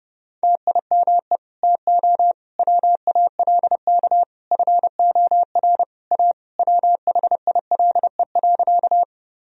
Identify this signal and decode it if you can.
Morse code, heard as a keyed tone, about 30 wpm, 700 Hz. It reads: TIME TO WALK FOR A WHILE.
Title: Sentences from Top 1000 Words - Review Part 1 - 30wpm